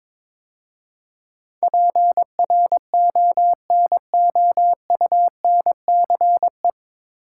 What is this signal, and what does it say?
Morse code, heard as a keyed tone, about 22 wpm, 700 Hz. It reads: PRONOUNCE
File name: 9 Letter Words - First 3 Letters Given - 22wpm